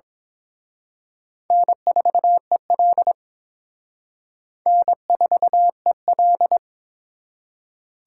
Morse code, heard as a keyed tone, about 26 wpm, 700 Hz. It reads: N4EL N4EL
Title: Speed Racing - USA Call Signs 22wpm